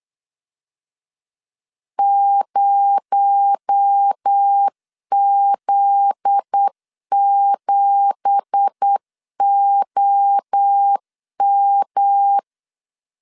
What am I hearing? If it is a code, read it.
Morse code, heard as a keyed tone, about 8 wpm, 800 Hz. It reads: 0Z7OM